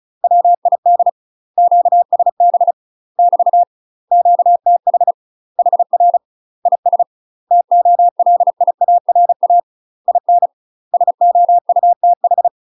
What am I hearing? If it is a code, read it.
Morse code, heard as a keyed tone, about 34 wpm, 700 Hz. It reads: WID QSB <BT> QTH HR IS TOLIARA IN SOUTH
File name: UK MASTER QSOs TRAINING/35WPM